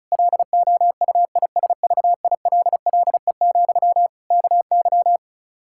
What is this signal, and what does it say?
Morse code, heard as a keyed tone, about 36 wpm, 700 Hz. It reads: LOUISVILLE, KY
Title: K1USN SST QSO Practice 35wpm